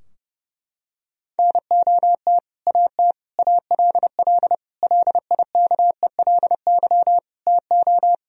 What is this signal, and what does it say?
Morse code, heard as a keyed tone, about 30 wpm, 700 Hz. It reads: NOT AT ALL LIKELY TO
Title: Sentences from Top 1000 Words - Review Part 4 - 30wpm